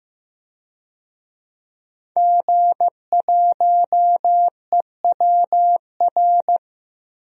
Morse code, heard as a keyed tone, about 15 wpm, 700 Hz. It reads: G1EWR
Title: DXCC-100 Call Signs - 5 Characters - 15wpm